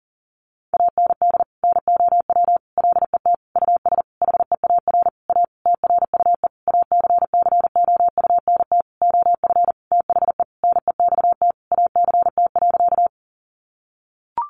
Morse code, heard as a keyed tone, about 37 wpm, 700 Hz. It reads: AND NOW LET US HEAR A TRUE ACCOUNT OF THE NEXT ACT.
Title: Sentences from Top 900 Words 40wpm